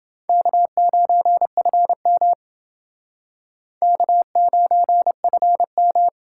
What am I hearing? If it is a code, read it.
Morse code, heard as a keyed tone, about 29 wpm, 700 Hz. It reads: K9FM K9FM